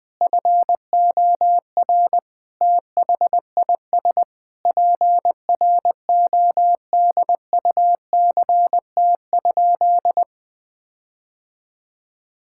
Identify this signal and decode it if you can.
Morse code, heard as a keyed tone, about 20 wpm, 700 Hz. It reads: FOR THIS PRODUCT?
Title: Sentences with 9-Word Phrases - Spoken Beginning - 20wpm